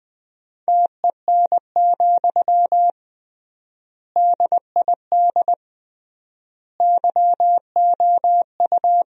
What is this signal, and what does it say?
Morse code, heard as a keyed tone, about 20 wpm, 700 Hz. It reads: TEN, DID YOU
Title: 3x Word-Spacing - Sentences from Top 400 Words - 20wpm